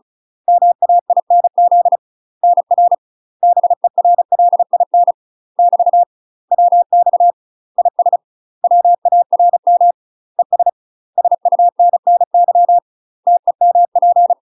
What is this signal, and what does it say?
Morse code, heard as a keyed tone, about 34 wpm, 700 Hz. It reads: MAINZ NR BERLIN <BT> WX IS WARM ES SUNNY TEMP
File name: UK MASTER QSOs TRAINING / 35WPM / UK MASTER QSOs 0042